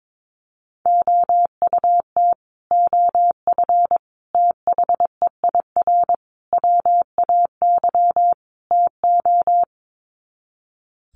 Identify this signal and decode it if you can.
Morse code, heard as a keyed tone, about 22 wpm, 700 Hz. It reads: OUT OF THEIR WAY TO
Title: Sets of 5 Words Encore2 22wpm